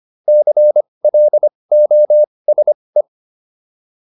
Morse code, heard as a keyed tone, about 25 wpm, spaced as for 20 wpm, 600 Hz. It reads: CLOSE